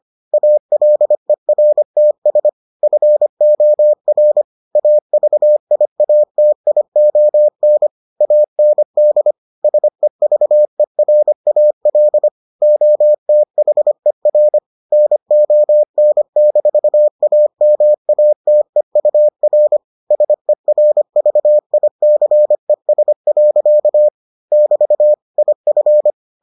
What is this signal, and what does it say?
Morse code, heard as a keyed tone, about 25 wpm, 600 Hz. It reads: ALERTS FOR AVIATION AND SEVERAL OTHER NON-AMATEUR SERVICES. <BT> IF